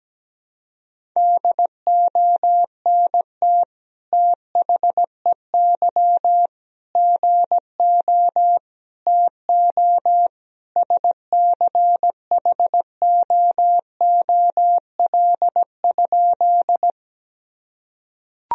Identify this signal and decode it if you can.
Morse code, heard as a keyed tone, about 17 wpm, 700 Hz. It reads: DONT THEY GO TO SCHOOL?
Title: Sentences from Top 1000 Words - Review Part 1 - 17wpm